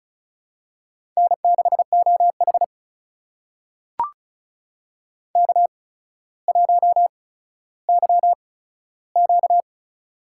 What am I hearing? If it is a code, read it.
Morse code, heard as a keyed tone, about 36 wpm, spaced as for 15 wpm, 700 Hz. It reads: N6OH K1YQ
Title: USA Call Signs - 4 Characters - 8x Character Spacing - 35wpm